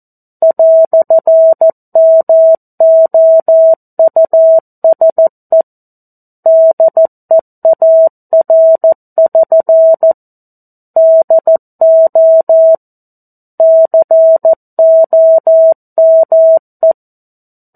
Morse code, heard as a keyed tone, about 14 wpm, 650 Hz. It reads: "MOUSE DEAR<SN> DO COME
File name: 02-Chapter-02 14 WPM